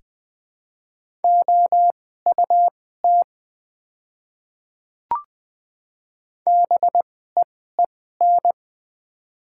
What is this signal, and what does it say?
Morse code, heard as a keyed tone, about 20 wpm, spaced as for 15 wpm, 690 Hz. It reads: OUT BEEN
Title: 2x Character Spacing - Top 200 Words - 20wpm